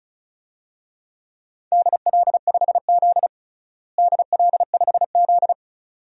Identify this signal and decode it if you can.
Morse code, heard as a keyed tone, about 34 wpm, 700 Hz. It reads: DL5Z DL5Z
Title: DXCC-100 Call Signs - 4 Characters - Repeated 2x - 35wpm